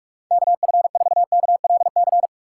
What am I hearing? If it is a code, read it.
Morse code, heard as a keyed tone, about 46 wpm, 700 Hz. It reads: KF4KLC
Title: USA Call Signs - 4x Character Spacing - 45wpm